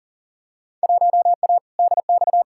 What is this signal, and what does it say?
Morse code, heard as a keyed tone, about 40 wpm, 700 Hz. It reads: 1A DX
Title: Field Day Exchanges Encore - 40wpm